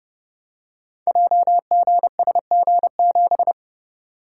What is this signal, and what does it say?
Morse code, heard as a keyed tone, about 29 wpm, 700 Hz. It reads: JGSG7
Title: Code Groups - 5 Letter:Numbers - 30wpm